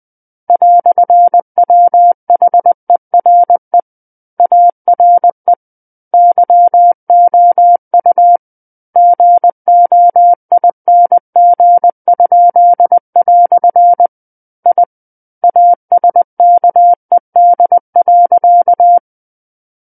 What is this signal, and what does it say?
Morse code, heard as a keyed tone, about 20 wpm, 700 Hz. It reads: "WHERE ARE YOU GOING?" I ASKED.
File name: 20 WPM